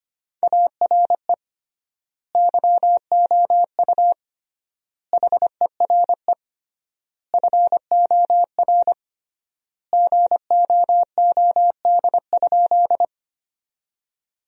Morse code, heard as a keyed tone, about 25 wpm, 700 Hz. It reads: ARE YOU HERE FOR GOOD?
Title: 3x Word-Spacing - Sentences from Top 200 Words - 25wpm